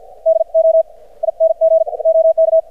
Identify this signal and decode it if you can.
Morse code, heard as a keyed tone, about 50 wpm, spaced as for 34 wpm, 590 Hz. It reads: DOING OK
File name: archived / CRSnet